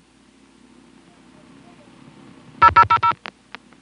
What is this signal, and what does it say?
DTMF, heard as telephone touch tones, tones of about 80 ms. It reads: 0000